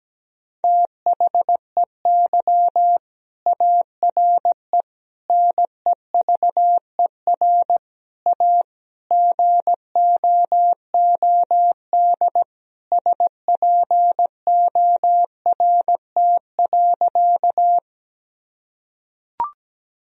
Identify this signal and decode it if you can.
Morse code, heard as a keyed tone, about 17 wpm, 700 Hz. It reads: THEY ARE NEVER A GOOD SPORT.
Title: Sentences from Top 1000 Words - Review Part 4 - 17wpm